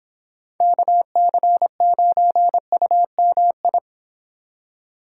Morse code, heard as a keyed tone, about 26 wpm, 700 Hz. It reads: KC9UMI